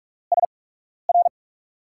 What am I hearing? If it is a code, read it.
Morse code, heard as a keyed tone, about 44 wpm, spaced as for 12 wpm, 700 Hz. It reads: SR
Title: USA Call Signs - 5 Characters - 8x Character Spacing - 45wpm